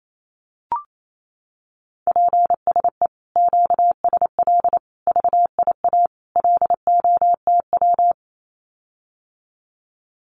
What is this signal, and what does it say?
Morse code, heard as a keyed tone, about 28 wpm, 700 Hz. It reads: PSE QSL VIA LOTW
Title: Common QSO Phrases Encore 28wpm